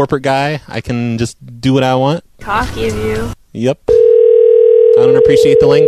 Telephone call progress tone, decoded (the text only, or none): ringback tone